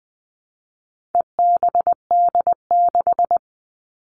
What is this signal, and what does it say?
Morse code, heard as a keyed tone, about 20 wpm, 700 Hz. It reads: EBD6
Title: All Cut Numbers - 4-Digit 20wpm